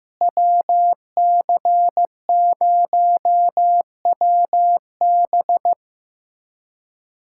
Morse code, heard as a keyed tone, about 15 wpm, 700 Hz. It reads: WC0WB
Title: USA Callsign DE Callsign - 5 Characters - 15wpm